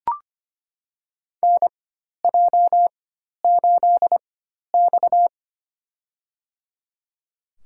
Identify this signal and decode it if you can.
Morse code, heard as a keyed tone, about 25 wpm, spaced as for 12 wpm, 700 Hz. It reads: NJ8X
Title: USA Call Signs - 4 Characters - 4x Character Spacing - 25wpm